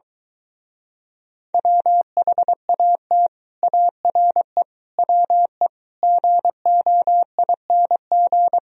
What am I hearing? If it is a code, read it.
Morse code, heard as a keyed tone, about 22 wpm, 700 Hz. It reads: WHAT ARE WE GOING